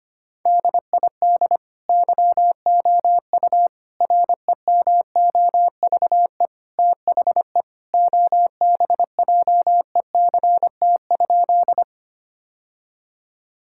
Morse code, heard as a keyed tone, about 25 wpm, 700 Hz. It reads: DID YOU REMOVE THE OBJECT?
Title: Sentences from Top 1000 Words - Review Part 5 - 25wpm